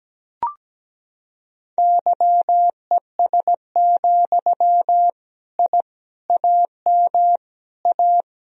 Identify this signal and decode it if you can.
Morse code, heard as a keyed tone, about 17 wpm, 700 Hz. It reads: YES, I AM A
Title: Sentences from Top 1000 Words - Review Part 2 - 17wpm